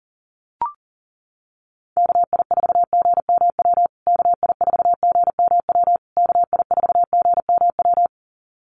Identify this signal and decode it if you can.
Morse code, heard as a keyed tone, about 40 wpm, 700 Hz. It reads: KI4GMW KI4GMW KI4GMW
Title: USA Call Signs Repeated 3x 40wpm